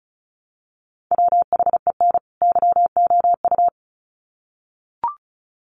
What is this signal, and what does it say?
Morse code, heard as a keyed tone, about 34 wpm, 700 Hz. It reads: WHEN YOU
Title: Sets of 2 Words 35wpm